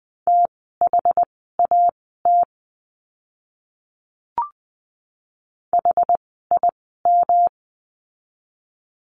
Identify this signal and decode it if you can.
Morse code, heard as a keyed tone, about 20 wpm, spaced as for 14 wpm, 700 Hz. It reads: THAT HIM
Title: 244 Lesson 39f - Letter z - Words 20wpm